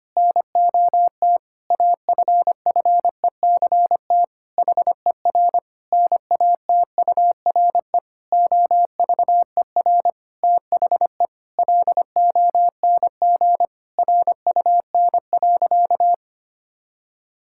Morse code, heard as a keyed tone, about 25 wpm, 700 Hz. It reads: NOT AFFECT HER NATURE OVER THE LONG RUN.
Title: Sentences from Top 1000 Words - Review Part 6 - 25wpm